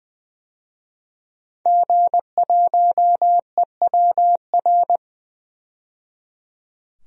Morse code, heard as a keyed tone, about 20 wpm, 700 Hz. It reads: G1EWR
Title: DXCC-100 Call Signs - 5 Characters - 20wpm